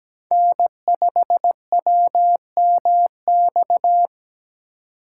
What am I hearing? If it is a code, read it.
Morse code, heard as a keyed tone, about 22 wpm, 700 Hz. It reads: N5WMX